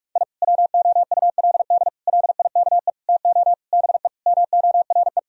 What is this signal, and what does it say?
Morse code, heard as a keyed tone, about 42 wpm, 700 Hz. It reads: I WOULD LIKE TO BE MORE
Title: Sentences from Top 1000 Words - Review Part 3 - 45wpm